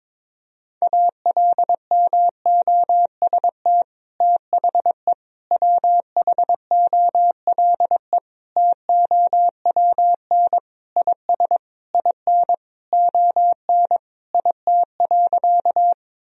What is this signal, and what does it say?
Morse code, heard as a keyed tone, about 22 wpm, 700 Hz. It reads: ALMOST THE WHOLE TOWN IS IN ON IT.
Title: Sentences from Top 1000 Words - Review Part 4 - 22wpm